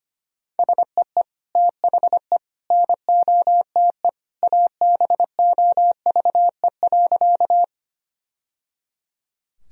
Morse code, heard as a keyed tone, about 25 wpm, 700 Hz. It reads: SEE THE NOTE ABOVE.
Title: Sentences from Top 1000 Words - Review Part 2 - 25wpm